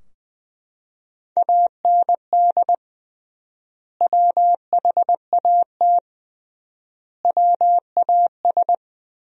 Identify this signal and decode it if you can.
Morse code, heard as a keyed tone, about 21 wpm, 700 Hz. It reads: AND WHAT WAS
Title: 3x Word-Spacing - Sentences from Top 100 Words - 20wpm